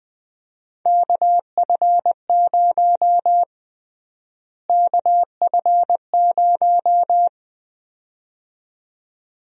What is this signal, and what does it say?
Morse code, heard as a keyed tone, about 20 wpm, 700 Hz. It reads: KF0 KF0
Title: USA Call Sign Prefixes - Repeated 2x - 20wpm